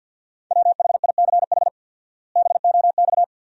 Morse code, heard as a keyed tone, about 46 wpm, 700 Hz. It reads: WHICH BOX
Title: 4x Word-Spacing - Sentences from Top 500 Words - 50wpm